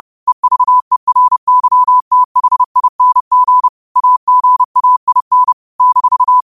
Morse code, heard as a keyed tone, about 30 wpm, 1000 Hz. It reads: EVERYTHING AGAIN -
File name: qotdfiles / 30wpm